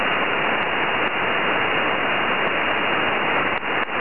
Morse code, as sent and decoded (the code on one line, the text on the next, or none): none